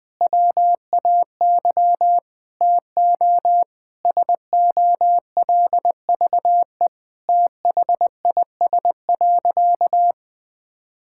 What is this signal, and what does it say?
Morse code, heard as a keyed tone, about 20 wpm, 700 Hz. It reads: WAY TO SOLVE THIS.
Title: Sentences with 4-Word Phrases - 20wpm